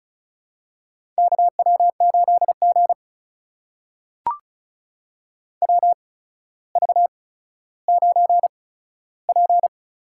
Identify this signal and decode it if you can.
Morse code, heard as a keyed tone, about 34 wpm, spaced as for 15 wpm, 700 Hz. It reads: KW8G WV9P
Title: USA Call Signs - 4 Characters - 8x Character Spacing - 35wpm